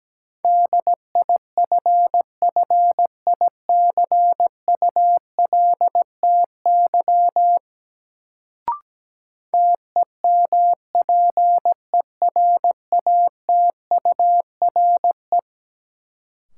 Morse code, heard as a keyed tone, about 17 wpm, 700 Hz. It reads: DIFFICULTY TEMPERATURE
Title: Sentences from Top 1500 Words - New Word Drill Alt - 17wpm